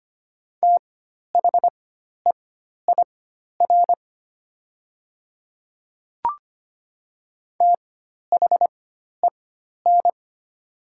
Morse code, heard as a keyed tone, about 25 wpm, spaced as for 12 wpm, 700 Hz. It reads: THEIR THEN
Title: 4x Character Spacing - Top 500 Words - 25wpm